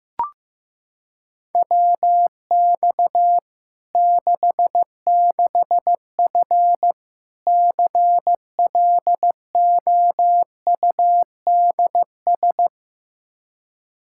Morse code, heard as a keyed tone, about 15 wpm, 700 Hz. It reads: WX 66F CLOUDS
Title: Practice QSO - Exchange 2 - Info - 15wpm